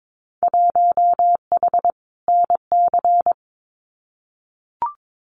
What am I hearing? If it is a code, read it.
Morse code, heard as a keyed tone, about 22 wpm, 700 Hz. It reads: 1H NC
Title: Winter Field Day Exchanges 22wpm